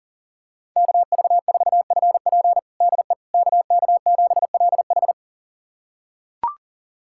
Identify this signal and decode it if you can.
Morse code, heard as a keyed tone, about 40 wpm, 700 Hz. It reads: KV4FP DE KK7LH
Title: USA Callsign DE Callsign - 5 Characters - 40wpm